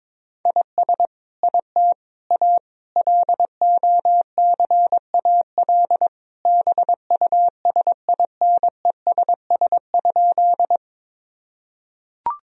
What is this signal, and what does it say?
Morse code, heard as a keyed tone, about 22 wpm, 700 Hz. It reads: IS IT A LOCAL BUSINESS?
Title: Sentences from Top 1000 Words - Review Part 3 - 22wpm